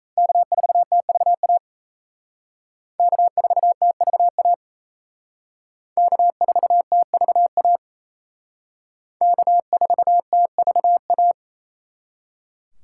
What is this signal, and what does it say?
Morse code, heard as a keyed tone, about 36 wpm, 700 Hz. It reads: K4TVA K4TVA K4TVA K4TVA